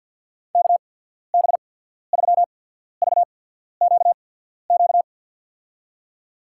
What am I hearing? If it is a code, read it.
Morse code, heard as a keyed tone, about 53 wpm, spaced as for 14 wpm, 690 Hz. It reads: KB3VQQ
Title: USA Call Signs - 6 Characters - 8x Character Spacing - 50wpm